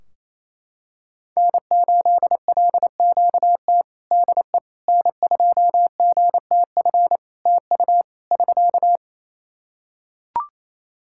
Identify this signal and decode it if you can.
Morse code, heard as a keyed tone, about 28 wpm, 700 Hz. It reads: N8LQT DE N2GTF TU <SK>
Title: K1USN SST QSO Practice 28wpm